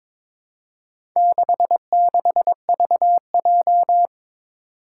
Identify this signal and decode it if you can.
Morse code, heard as a keyed tone, about 22 wpm, 700 Hz. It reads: 66VJ
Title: Four Challenging Letter-Number Combinations 22wpm